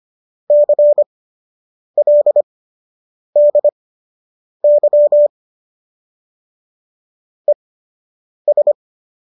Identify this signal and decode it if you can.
Morse code, heard as a keyed tone, about 25 wpm, spaced as for 8 wpm, 590 Hz. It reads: CLDY ES